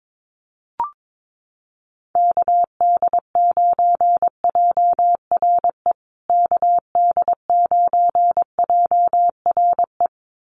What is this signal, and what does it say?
Morse code, heard as a keyed tone, about 22 wpm, 700 Hz. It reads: KD9JRE KD9JRE
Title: USA Call Signs Repeated 2x 22wpm